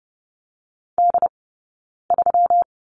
Morse code, heard as a keyed tone, about 30 wpm, 700 Hz. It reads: D 3
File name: Lesson 20d - Number 3 - Two Characters 30wpm